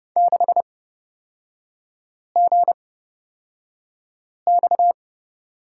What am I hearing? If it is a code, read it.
Morse code, heard as a keyed tone, about 31 wpm, spaced as for 5 wpm, 690 Hz. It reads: 6GX
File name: Lesson 38e - Letter q - Three Characters 30wpm